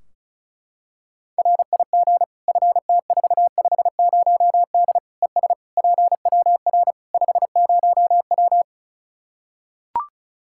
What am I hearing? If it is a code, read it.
Morse code, heard as a keyed tone, about 36 wpm, 700 Hz. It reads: RIG FT450D ES PWR 50W
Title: K1USN SST QSO Practice 35wpm